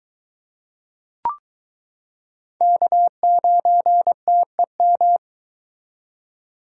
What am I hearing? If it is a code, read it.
Morse code, heard as a keyed tone, about 23 wpm, 700 Hz. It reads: K9TEM